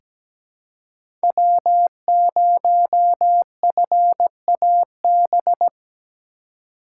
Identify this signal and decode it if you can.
Morse code, heard as a keyed tone, about 17 wpm, 700 Hz. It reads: W0FAB
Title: USA Call Signs - 5 Characters - 3x Character Spacing - 17wpm